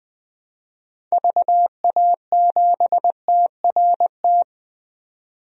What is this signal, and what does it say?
Morse code, heard as a keyed tone, about 20 wpm, 700 Hz. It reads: VA7TRT
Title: DXCC-100 Call Signs - 6 Characters - 20wpm